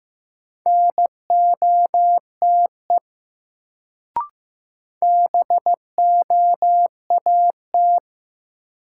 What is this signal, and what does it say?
Morse code, heard as a keyed tone, about 15 wpm, 700 Hz. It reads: NOTE BOAT
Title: Sentences from Top 500 Words - New Word Drill - 15wpm